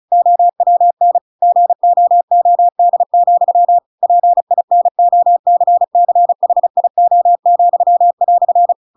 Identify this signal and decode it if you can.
Morse code, heard as a keyed tone, about 37 wpm, 700 Hz. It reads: OWN GOOD, PINOCCHIO,"